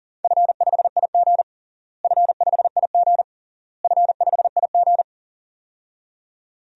Morse code, heard as a keyed tone, about 40 wpm, 700 Hz. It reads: F5IG F5IG F5IG
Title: DXCC-100 Call Signs - 4 Characters - Repeated 3x - 40wpm